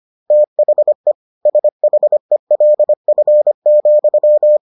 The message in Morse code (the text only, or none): THE SHELF,